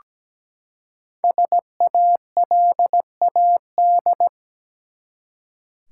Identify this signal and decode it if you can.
Morse code, heard as a keyed tone, about 17 wpm, 700 Hz. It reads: SALAD
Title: Five Letter Words - 17wpm